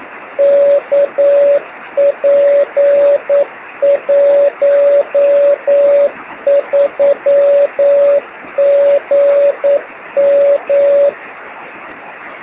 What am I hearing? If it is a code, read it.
Morse code, heard as a keyed tone, about 9 wpm, 570 Hz. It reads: KP13GM
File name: sm7lcb / oh6shf 23cm